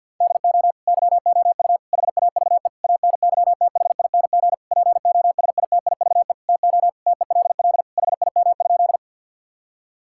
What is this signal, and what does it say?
Morse code, heard as a keyed tone, about 50 wpm, 700 Hz. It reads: DO YOU HAVE ANYTHING POSITIVE TO TELL HIM?
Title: Sentences from Top 1000 Words - Review Part 5 - 50wpm